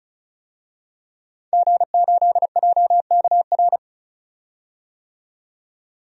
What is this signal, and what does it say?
Morse code, heard as a keyed tone, about 34 wpm, 700 Hz. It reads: G8JKR
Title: DXCC-100 Call Signs - 5 Characters - 35wpm